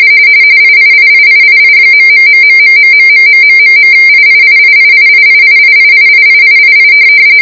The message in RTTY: IIIIIII............IIIIIIIIIIIIIIIII